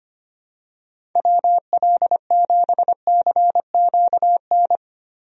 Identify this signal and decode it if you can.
Morse code, heard as a keyed tone, about 25 wpm, 700 Hz. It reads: WL7CQN
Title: USA Call Signs - 6 Characters - 25wpm